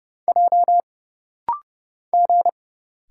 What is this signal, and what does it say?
Morse code, heard as a keyed tone, about 30 wpm, 700 Hz. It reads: J G
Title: Single Letters - Rapid-Fire 30wpm